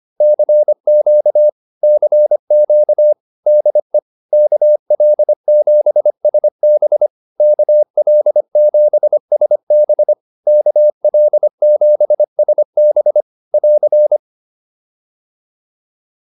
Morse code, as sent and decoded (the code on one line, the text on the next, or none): -.-. --.- / -.-. --.- / -.. / -.- .-.. --... ... -... / -.- .-.. --... ... -... / -.- .-.. --... ... -... / .-.-.
CQ CQ DE KL7SB KL7SB KL7SB <AR>